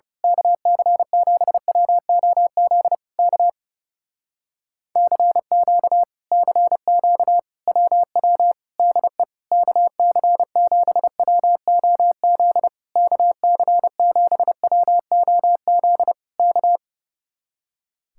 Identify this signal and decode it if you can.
Morse code, heard as a keyed tone, about 28 wpm, 700 Hz. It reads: KC7WOZ K CQ CQ WW DE KC7WOZ KC7WOZ K
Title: Speed-Racing Common QSO Phrases - 30wpm